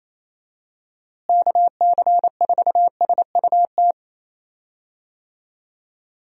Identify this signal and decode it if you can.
Morse code, heard as a keyed tone, about 28 wpm, 700 Hz. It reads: KC4SUT
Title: USA Call Signs - 6 Characters - Repeated 2x - 28wpm